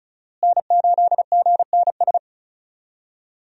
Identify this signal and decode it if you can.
Morse code, heard as a keyed tone, about 36 wpm, 700 Hz. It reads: N8GNS
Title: USA Call Signs - 5 Characters - Repeated 2x - 35wpm